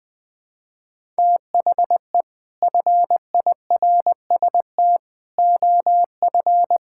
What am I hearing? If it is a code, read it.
Morse code, heard as a keyed tone, about 20 wpm, 700 Hz. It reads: THE FIRST OF
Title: Sets of 6 Words - 20wpm